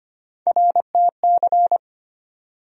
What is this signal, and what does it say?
Morse code, heard as a keyed tone, about 25 wpm, 700 Hz. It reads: RTC
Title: USA Call Sign Postfixes - 3 Letters 25wpm